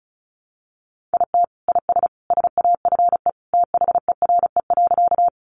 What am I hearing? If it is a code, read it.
Morse code, heard as a keyed tone, about 34 wpm, 700 Hz. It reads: IT IS SAFE THERE.
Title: Easy Phrases 35wpm